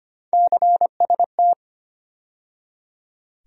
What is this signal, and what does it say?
Morse code, heard as a keyed tone, about 25 wpm, 700 Hz. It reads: CST
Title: USA Call Sign Postfixes - 3 Letters 25wpm